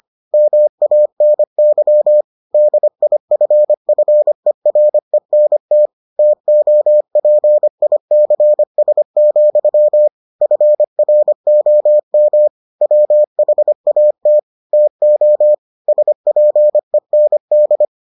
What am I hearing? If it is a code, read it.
Morse code, heard as a keyed tone, about 25 wpm, 600 Hz. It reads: MANY DIFFERENT TOPICS, FROM WHAT TO SPEND